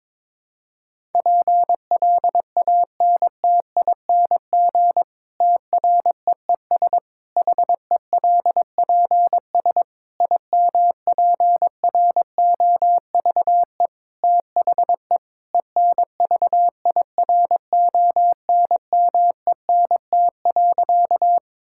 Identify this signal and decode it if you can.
Morse code, heard as a keyed tone, about 22 wpm, 700 Hz. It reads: PLANTING TREES HELPS IMPROVE THE ENVIRONMENT.
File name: Difficult Sentences with Hints - 22wpm